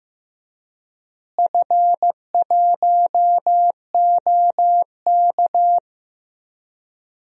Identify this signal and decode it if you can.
Morse code, heard as a keyed tone, about 15 wpm, 700 Hz. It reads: F1OK
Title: DXCC-100 Call Signs - 4 Characters - Repeated 3x - 15wpm